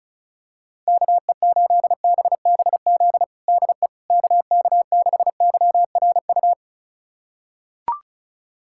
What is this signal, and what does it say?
Morse code, heard as a keyed tone, about 34 wpm, 700 Hz. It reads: KE8BBZ DE KK6YRU
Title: USA Callsign DE Callsign - 6 Characters - 35wpm